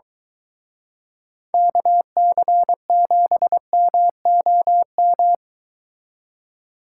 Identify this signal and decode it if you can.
Morse code, heard as a keyed tone, about 22 wpm, 700 Hz. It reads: KC7MOM